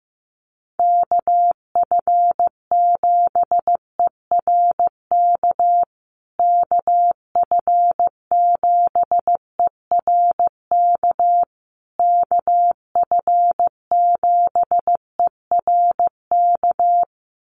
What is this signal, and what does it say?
Morse code, heard as a keyed tone, about 14 wpm, 700 Hz. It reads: KF7ERK KF7ERK KF7ERK
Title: USA Call Signs Repeated 3x 15wpm